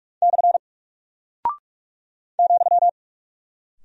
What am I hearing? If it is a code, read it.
Morse code, heard as a keyed tone, about 44 wpm, 700 Hz. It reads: / ,
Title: Single Character (Letter:Number:Punctuation:Prosign) Alt 45wpm